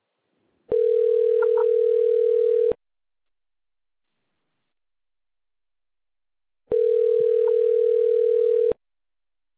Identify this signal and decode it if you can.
Ringback tone (North America), 440 + 480 Hz.